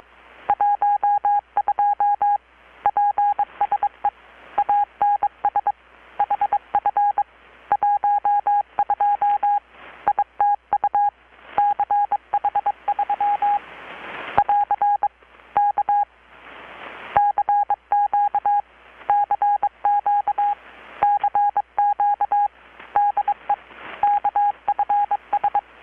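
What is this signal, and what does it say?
Morse code, heard as a keyed tone, about 23 wpm, 820 Hz. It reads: 12 PSE ANS HF 12 ITU CH3 <AR> K CQ CQ CQ DE KFS
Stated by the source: Morse from KPH's Night of Nights XX maritime radiotelegraphy event, received in Chiapas, Mexico about 3,800 km away on a Kenwood R-600 with a grounded 10 m long-wire antenna.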